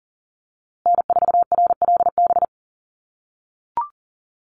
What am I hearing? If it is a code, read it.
Morse code, heard as a keyed tone, about 40 wpm, 700 Hz. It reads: N4RLB
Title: USA Call Signs Repeated 2x 40wpm